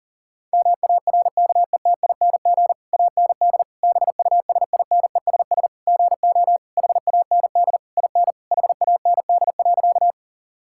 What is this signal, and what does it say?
Morse code, heard as a keyed tone, about 40 wpm, 700 Hz. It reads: MARKETING AND BUSINESS GO HAND IN HAND.
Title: Sentences from Top 1000 Words - Review Part 5 - 40wpm